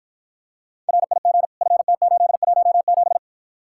Morse code, heard as a keyed tone, about 53 wpm, 700 Hz. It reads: RIG FT817